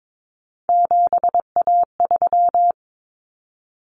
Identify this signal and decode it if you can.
Morse code, heard as a keyed tone, about 22 wpm, 700 Hz. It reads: 7A3
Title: Common Cut Numbers - 3-Digit 22wpm